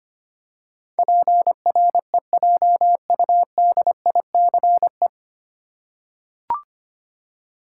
Morse code, heard as a keyed tone, about 25 wpm, 690 Hz. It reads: PREJUDICE
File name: practice / 25wpm / 9 Letter Words - First 3 Letters Given - 25wpm